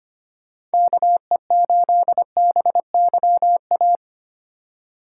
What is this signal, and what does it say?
Morse code, heard as a keyed tone, about 25 wpm, 700 Hz. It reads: KE8BYA
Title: USA Call Signs - 6 Characters - 4x Character Spacing - 25wpm